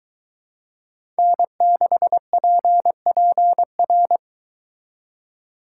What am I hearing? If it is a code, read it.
Morse code, heard as a keyed tone, about 23 wpm, 700 Hz. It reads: N6PPR